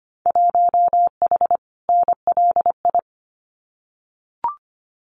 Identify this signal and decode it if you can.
Morse code, heard as a keyed tone, about 25 wpm, 700 Hz. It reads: 1H NLI
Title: Winter Field Day Exchanges 25wpm